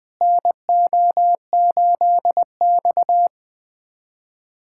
Morse code, heard as a keyed tone, about 20 wpm, 700 Hz. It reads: NO8X
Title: USA Callsign DE Callsign - 4 Characters - 20wpm